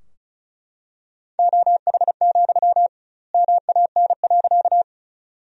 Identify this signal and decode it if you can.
Morse code, heard as a keyed tone, about 34 wpm, 700 Hz. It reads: OH, MAN.
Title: 2x Word-Spacing - Sentences from Top 500 Words - 35wpm